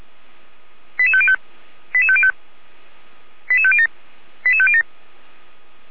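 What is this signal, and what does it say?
Five-tone selective call (ZVEI), heard as quick runs of five tones, tones of about 70 ms. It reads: 88505 88505 88507 88507
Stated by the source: ZVEI five-tone call 88405 received by SDR.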